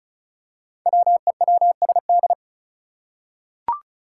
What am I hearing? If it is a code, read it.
Morse code, heard as a keyed tone, about 34 wpm, 700 Hz. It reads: WEWSD
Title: Five Challenging Letter-Number Combinations 35wpm